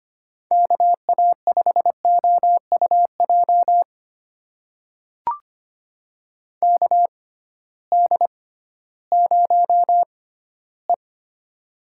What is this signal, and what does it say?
Morse code, heard as a keyed tone, about 25 wpm, spaced as for 14 wpm, 700 Hz. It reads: KA5OUJ KD0E
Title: USA Call Signs - 6x Character Spacing - 25wpm